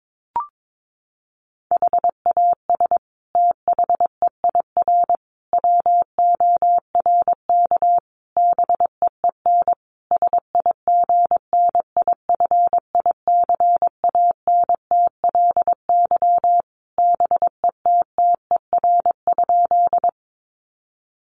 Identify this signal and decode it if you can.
Morse code, heard as a keyed tone, about 22 wpm, 700 Hz. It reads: HAS THEIR WORK BEEN SIGNIFICANTLY BETTER?
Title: Sentences from Top 1500 Words 22wpm